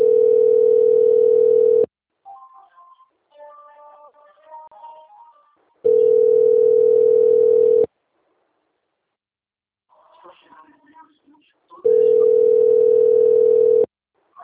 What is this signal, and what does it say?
Ringback tone (North America), 440 + 480 Hz, with speech.
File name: out- - - - - .48327